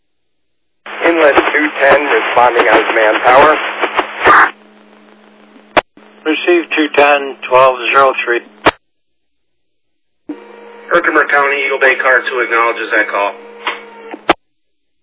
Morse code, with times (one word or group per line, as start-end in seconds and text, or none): none